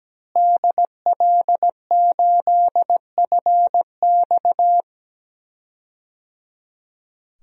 Morse code, heard as a keyed tone, about 17 wpm, 700 Hz. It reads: DL8FX
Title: DXCC-100 Call Signs - 5 Characters - Repeated 3x - 17wpm